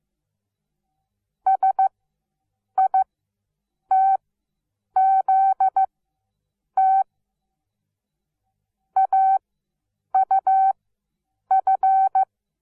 Morse code, heard as a keyed tone, about 15 wpm, spaced as for 8 wpm, 770 Hz. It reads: SITZT AUF
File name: Onlinelehrgang / morsen / DJ4UF Morsen17